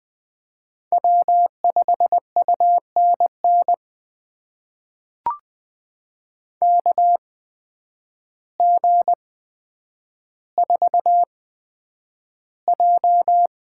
Morse code, heard as a keyed tone, about 20 wpm, spaced as for 9 wpm, 700 Hz. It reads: W5UNN KG4J
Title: USA Call Signs - 8x Character Spacing - 20wpm